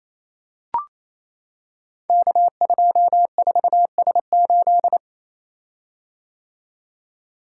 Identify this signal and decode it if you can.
Morse code, heard as a keyed tone, about 29 wpm, 700 Hz. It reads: K24S8
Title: Code Groups - 5 Letter:Numbers - 28wpm